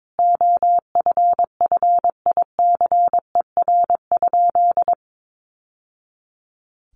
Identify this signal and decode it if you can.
Morse code, heard as a keyed tone, about 22 wpm, 700 Hz. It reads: OFFICER?
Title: Sentences from Top 1200 Words 22wpm